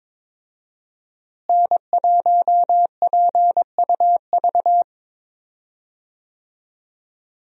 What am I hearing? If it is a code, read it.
Morse code, heard as a keyed tone, about 22 wpm, 700 Hz. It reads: N1PUV
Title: USA Call Signs - 5 Characters - Repeated 2x - 22wpm